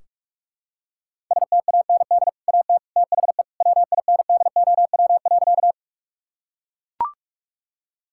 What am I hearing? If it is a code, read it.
Morse code, heard as a keyed tone, about 41 wpm, 700 Hz. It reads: STAND AT THE WINDOW.
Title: Sentences from Top 1600 Words 45wpm